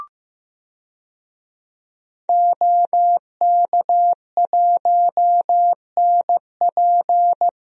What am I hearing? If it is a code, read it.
Morse code, heard as a keyed tone, about 15 wpm, 700 Hz. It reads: OK1NP
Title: DXCC-100 Call Signs - 5 Characters - Repeated 3x - 15wpm